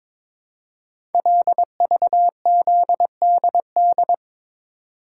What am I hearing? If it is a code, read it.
Morse code, heard as a keyed tone, about 23 wpm, 700 Hz. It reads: LVZDD
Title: Code Groups - 5 Letters - 22wpm